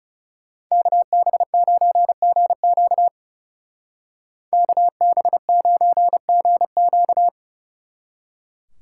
Morse code, heard as a keyed tone, about 36 wpm, 700 Hz. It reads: KB9GQ KB9GQ
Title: Speed Racing - USA 2x2 Call Signs 30wpm